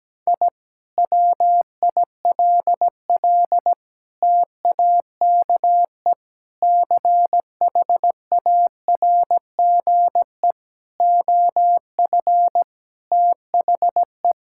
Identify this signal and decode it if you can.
Morse code, heard as a keyed tone, about 17 wpm, 700 Hz. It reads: I WILL TAKE CHARGE OF THE
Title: Sentences from Top 1000 Words - Review Part 6 - 17wpm